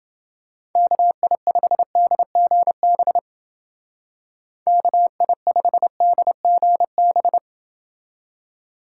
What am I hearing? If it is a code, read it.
Morse code, heard as a keyed tone, about 28 wpm, 700 Hz. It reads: KI5DGB KI5DGB